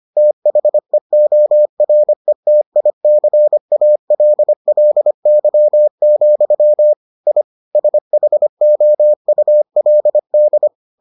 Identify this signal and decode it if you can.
Morse code, heard as a keyed tone, about 25 wpm, 600 Hz. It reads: THEORETICALLY, I SHOULD